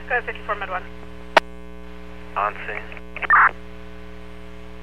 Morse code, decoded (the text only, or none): none